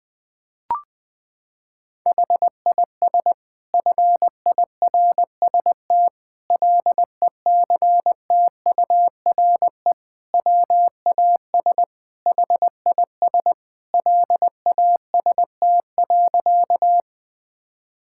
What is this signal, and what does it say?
Morse code, heard as a keyed tone, about 21 wpm, 700 Hz. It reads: HIS FIRST LECTURE WAS HIS LAST.
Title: Sentences from Top 1600 Words 20wpm